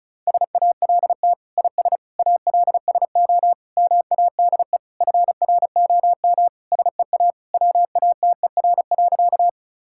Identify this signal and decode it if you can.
Morse code, heard as a keyed tone, about 34 wpm, 700 Hz. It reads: SALT IS ALSO MADE FROM SEA WATER.
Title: Sentences from Top 1000 Words - Review Part 5 - 35wpm